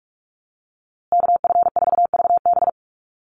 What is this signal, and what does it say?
Morse code, heard as a keyed tone, about 44 wpm, 700 Hz. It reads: KF4VB
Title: USA Call Signs Repeated 2x 45wpm